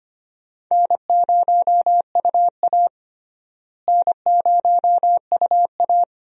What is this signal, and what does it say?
Morse code, heard as a keyed tone, about 25 wpm, 700 Hz. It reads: N0UA N0UA
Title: USA Call Signs - 4 Characters - Repeated 3x - 25wpm